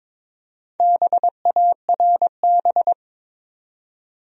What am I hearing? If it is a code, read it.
Morse code, heard as a keyed tone, about 22 wpm, 700 Hz. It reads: BARB
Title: USA Common Names - 22wpm